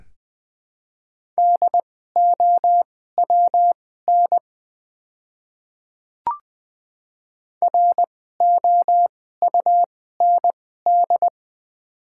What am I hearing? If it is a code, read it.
Morse code, heard as a keyed tone, about 21 wpm, spaced as for 15 wpm, 700 Hz. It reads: DOWN ROUND
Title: 2x Character Spacing - Top 200 Words - 20wpm